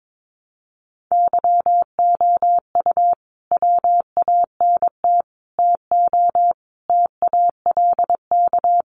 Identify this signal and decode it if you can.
Morse code, heard as a keyed tone, about 22 wpm, 700 Hz. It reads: YOU WANT TO TALK
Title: Sets of 5 Words Encore 22wpm